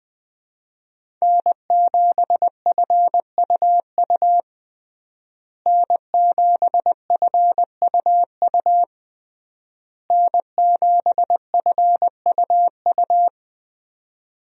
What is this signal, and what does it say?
Morse code, heard as a keyed tone, about 20 wpm, 700 Hz. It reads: N7FUU N7FUU N7FUU
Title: USA Call Signs - 5 Characters - Repeated 3x - 20wpm